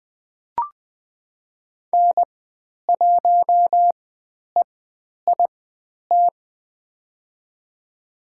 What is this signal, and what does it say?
Morse code, heard as a keyed tone, about 20 wpm, spaced as for 10 wpm, 700 Hz. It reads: N1EIT
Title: 102 Lesson 17g - Letter u - Call Signs 20wpm